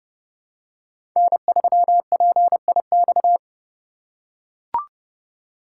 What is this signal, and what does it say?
Morse code, heard as a keyed tone, about 30 wpm, 700 Hz. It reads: N3PIX
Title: USA Call Signs - 5 Characters - 4x Character Spacing - 30wpm